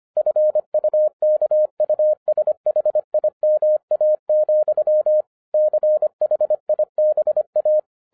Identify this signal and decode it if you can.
Morse code, heard as a keyed tone, about 25 wpm, 610 Hz. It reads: FUKUSHIMA, CHIBA
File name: 2011-07-mp3 / n34 THE 3.11 2 25wpm